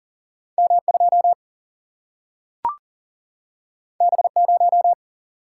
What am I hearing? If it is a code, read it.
Morse code, heard as a keyed tone, about 40 wpm, 700 Hz. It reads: M2 B0
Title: Single-Letter Single-Number 40wpm